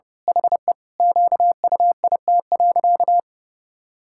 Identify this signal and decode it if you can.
Morse code, heard as a keyed tone, about 30 wpm, 700 Hz. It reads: HE QUIT.
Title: Pangram Challenge v2 - 30wpm